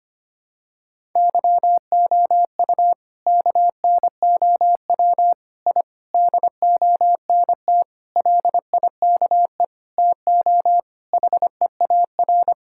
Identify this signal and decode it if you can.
Morse code, heard as a keyed tone, about 25 wpm, 700 Hz. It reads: YOU KNOW I DONT LIKE TO HEAR
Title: Sentences from Top 1000 Words - Review Part 6 - 25wpm